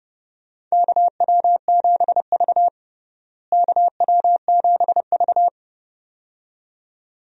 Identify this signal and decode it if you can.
Morse code, heard as a keyed tone, about 30 wpm, 700 Hz. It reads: KW7V KW7V
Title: USA Call Signs - 4 Characters - Repeated 2x - 30wpm